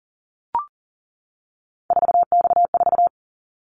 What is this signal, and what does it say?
Morse code, heard as a keyed tone, about 37 wpm, 700 Hz. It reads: 4X4
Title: Callsign Prefixes for Top 50 DXCC - 40wpm